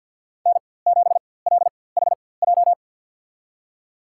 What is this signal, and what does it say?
Morse code, heard as a keyed tone, about 53 wpm, spaced as for 24 wpm, 690 Hz. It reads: N7LHJ
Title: USA Call Signs - 5 Characters - 4x Character Spacing - 50wpm